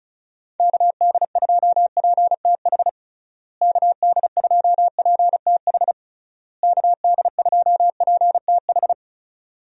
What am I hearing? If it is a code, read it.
Morse code, heard as a keyed tone, about 36 wpm, 700 Hz. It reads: KD2PTH KD2PTH KD2PTH
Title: USA Call Signs - 6 Characters - Repeated 3x - 35wpm